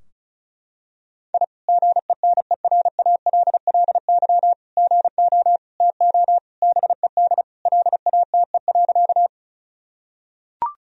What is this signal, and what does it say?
Morse code, heard as a keyed tone, about 34 wpm, 700 Hz. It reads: I GENERALLY GO TO BED LATE.
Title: Sentences from Top 1000 Words - Review Part 4 - 35wpm